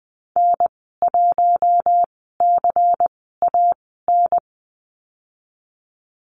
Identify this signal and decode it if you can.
Morse code, heard as a keyed tone, about 20 wpm, spaced as for 15 wpm, 700 Hz. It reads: N1CAN
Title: 122 Lesson 20g - Number 3 - Call Signs 20wpm